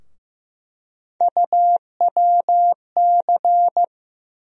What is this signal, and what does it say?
Morse code, heard as a keyed tone, about 15 wpm, 700 Hz. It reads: UWC
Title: USA Call Sign Postfixes - 3 Letters 15wpm